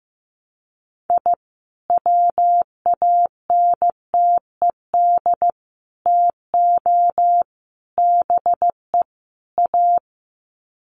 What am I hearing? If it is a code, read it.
Morse code, heard as a keyed tone, about 15 wpm, 700 Hz. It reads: I WANTED TO BE A
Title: Sets of 5 Words Encore2 15wpm